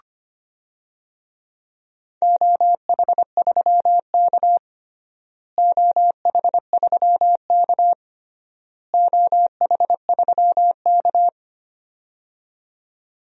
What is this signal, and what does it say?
Morse code, heard as a keyed tone, about 25 wpm, 700 Hz. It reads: OH3K OH3K OH3K
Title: DXCC-100 Call Signs - 4 Characters - Repeated 3x - 25wpm